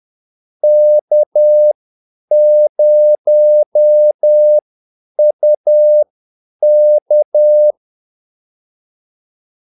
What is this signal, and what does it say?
Morse code, heard as a keyed tone, about 10 wpm, spaced as for 8 wpm, 600 Hz. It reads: K0UK
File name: Callsigns-10-8wpm